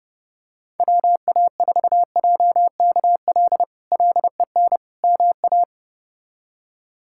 Morse code, heard as a keyed tone, about 30 wpm, 700 Hz. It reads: WA4JKL LEN MA
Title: K1USN SST QSO Practice - 30wpm